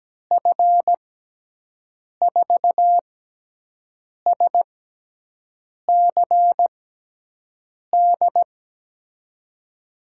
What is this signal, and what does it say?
Morse code, heard as a keyed tone, about 17 wpm, spaced as for 6 wpm, 700 Hz. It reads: F4SCD
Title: USA Call Signs - 6x Character Spacing - 17wpm